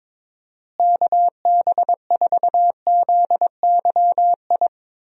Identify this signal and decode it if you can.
Morse code, heard as a keyed tone, about 23 wpm, 700 Hz. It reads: KB4ZYI
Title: USA Call Signs - 6 Characters - Repeated 2x - 22wpm